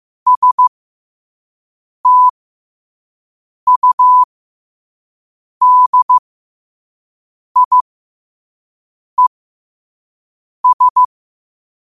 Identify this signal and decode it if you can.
Morse code, heard as a keyed tone, about 15 wpm, spaced as for 5 wpm, 1000 Hz. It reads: STUDIES
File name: qotdfiles / 5wpm